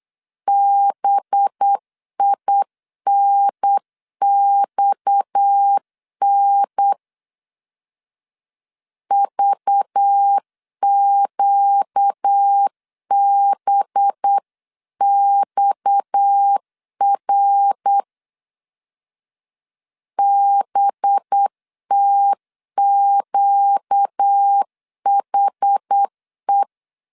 Morse code, heard as a keyed tone, about 8 wpm, 800 Hz. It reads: BINXN VQBXR BTQHE
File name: 8wpm